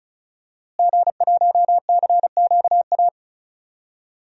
Morse code, heard as a keyed tone, about 36 wpm, 700 Hz. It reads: G1CQA
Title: DXCC-100 Call Signs - 5 Characters - Repeated 3x - 35wpm